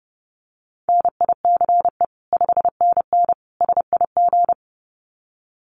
Morse code, heard as a keyed tone, about 30 wpm, 700 Hz. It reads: NICE 5NN SIG
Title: Common QSO Phrases 30wpm